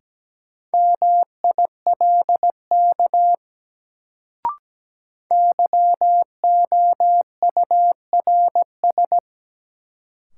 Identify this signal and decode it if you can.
Morse code, heard as a keyed tone, about 17 wpm, 700 Hz. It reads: MILK YOURS
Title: Sentences from Top 1500 Words - New Word Drill - 17wpm